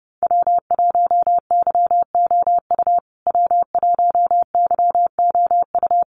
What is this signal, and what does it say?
Morse code, heard as a keyed tone, about 30 wpm, 700 Hz. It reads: W1YOU W1YOU
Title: USA Call Signs Repeated 3x 30wpm